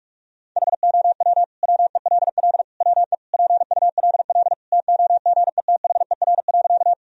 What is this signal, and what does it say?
Morse code, heard as a keyed tone, about 46 wpm, 700 Hz. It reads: HOW WELL WE PULL TOGETHER.
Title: Sentences from Top 1000 Words - Review Part 2 - 45wpm